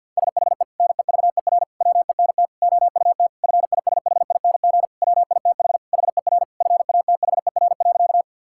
Morse code, heard as a keyed tone, about 50 wpm, 700 Hz. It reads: SHE NEVER WENT OUT FISHING WITH HER FATHER.
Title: Sentences from Top 1000 Words - Review Part 5 - 50wpm